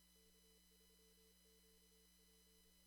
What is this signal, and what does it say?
Morse code, heard as a keyed tone, about 27 wpm, 440 Hz, with noise. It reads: K3QP
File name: CW / kt5j wpx cw-op-k5pi